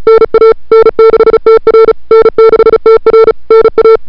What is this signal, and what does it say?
Morse code, heard as a keyed tone, about 35 wpm, 440 Hz. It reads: NA N6TR N6TR NA
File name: n6tr / Fab2007CWSprintHour1